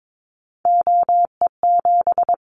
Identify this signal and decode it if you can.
Morse code, heard as a keyed tone, about 22 wpm, 700 Hz. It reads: OE7
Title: Callsign Prefixes for Top 100 DXCC - 22wpm